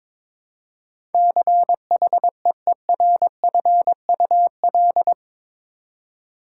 Morse code, heard as a keyed tone, about 22 wpm, 700 Hz. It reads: CHEERFUL
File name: practice / 22wpm / Eight Letter Words - 22wpm